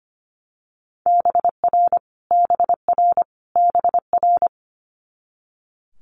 Morse code, heard as a keyed tone, about 25 wpm, 700 Hz. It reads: BR BR BR
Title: Multi-letter Phonemes Repeated 3x 25wpm